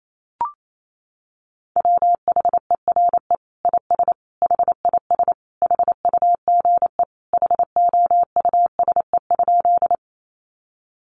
Morse code, heard as a keyed tone, about 28 wpm, 700 Hz. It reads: WHERE IS HIS HUGE HOUSE?
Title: Sentences from Top 1100 Words 28wpm